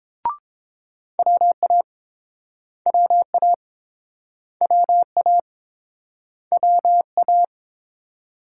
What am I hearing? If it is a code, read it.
Morse code, heard as a keyed tone, about 30 wpm, 700 Hz. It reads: WA WA WA WA